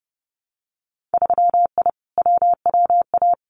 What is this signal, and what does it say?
Morse code, heard as a keyed tone, about 30 wpm, 700 Hz. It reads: 3I WWA
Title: Winter Field Day Exchanges Encore 30wpm